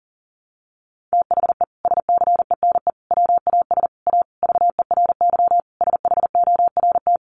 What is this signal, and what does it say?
Morse code, heard as a keyed tone, about 40 wpm, 700 Hz. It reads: THE SCENE WAS A VERY SHORT
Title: Sentences from Top 1100 Words 40wpm